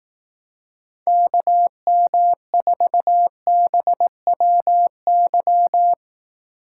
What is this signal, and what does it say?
Morse code, heard as a keyed tone, about 18 wpm, 700 Hz. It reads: KM4BWY
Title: Speed Racing - USA 2x3 Call Signs 15wpm